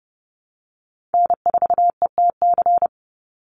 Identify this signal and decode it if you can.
Morse code, heard as a keyed tone, about 30 wpm, 700 Hz. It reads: N4ETC
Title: Call Signs - 30wpm